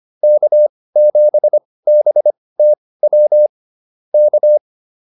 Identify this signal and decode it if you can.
Morse code, heard as a keyed tone, about 25 wpm, spaced as for 18 wpm, 600 Hz. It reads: K7BTW K